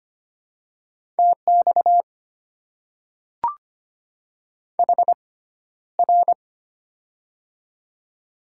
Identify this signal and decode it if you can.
Morse code, heard as a keyed tone, about 25 wpm, spaced as for 13 wpm, 700 Hz. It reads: TX HR
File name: Top 90 QSO Elements - 6x Character Spacing - 25wpm